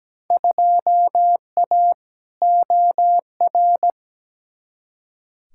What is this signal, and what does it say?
Morse code, heard as a keyed tone, about 17 wpm, 700 Hz. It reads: 2A OR
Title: Field Day Exchanges - 17wpm